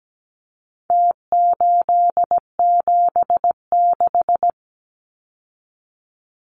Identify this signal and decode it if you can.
Morse code, heard as a keyed tone, about 17 wpm, 700 Hz. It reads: T876
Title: Numbers - Contest Style - 4-Digit 17wpm